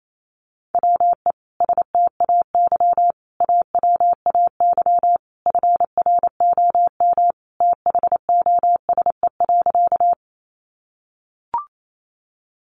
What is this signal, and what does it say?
Morse code, heard as a keyed tone, about 28 wpm, 700 Hz. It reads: WE STAY AWAY FROM THOSE.
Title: A Sound Of Thunder 28wpm